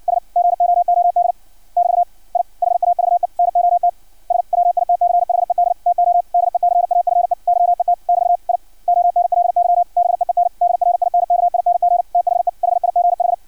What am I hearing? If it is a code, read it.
Morse code, heard as a keyed tone, about 56 wpm, 700 Hz. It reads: S GOOD <BT> I HAVE NOT SWITCHED TO DECAFE YET <BT> I ONLY BEEN DRINKING THE HIGH
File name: archived / CRSnet